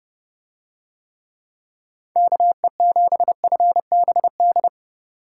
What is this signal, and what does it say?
Morse code, heard as a keyed tone, about 27 wpm, 700 Hz. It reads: KE7FBD